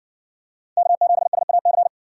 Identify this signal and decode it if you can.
Morse code, heard as a keyed tone, about 53 wpm, 700 Hz. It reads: K7SAC